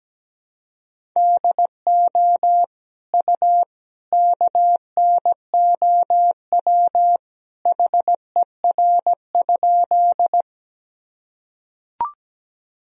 Morse code, heard as a keyed tone, about 17 wpm, 700 Hz. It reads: DO U KNOW HER?
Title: Sentences from Top 100 Words and Abbreviations 17wpm